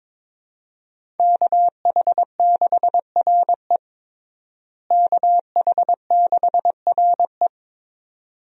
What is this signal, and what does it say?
Morse code, heard as a keyed tone, about 22 wpm, 700 Hz. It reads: KH6RE KH6RE
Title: USA Call Signs - 5 Characters - Repeated 2x - 22wpm